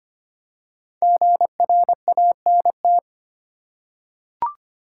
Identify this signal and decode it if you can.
Morse code, heard as a keyed tone, about 25 wpm, 700 Hz. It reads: GRANT
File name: 5 Letter Words - First 3 Letters Given - 25wpm